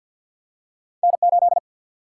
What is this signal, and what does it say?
Morse code, heard as a keyed tone, about 54 wpm, 700 Hz. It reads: N 8
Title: Single-Letter Single-Number 50wpm